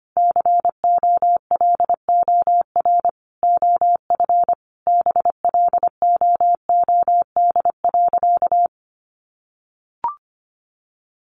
Morse code, heard as a keyed tone, about 25 wpm, 700 Hz. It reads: COLOR OF BLOOD.
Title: A Sound Of Thunder 25wpm